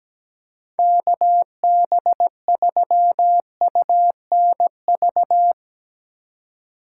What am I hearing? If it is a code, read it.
Morse code, heard as a keyed tone, about 17 wpm, 700 Hz. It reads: KB3UNV
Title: USA Call Signs - 2x Character Spacing - 17wpm